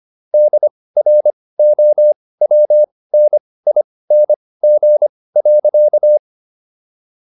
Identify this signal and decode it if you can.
Morse code, heard as a keyed tone, about 25 wpm, spaced as for 18 wpm, 600 Hz. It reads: DROWNING.